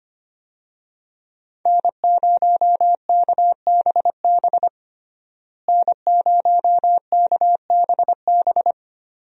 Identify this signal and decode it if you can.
Morse code, heard as a keyed tone, about 25 wpm, 700 Hz. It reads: N0KBB N0KBB
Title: USA Call Signs - 5 Characters - Repeated 2x - 25wpm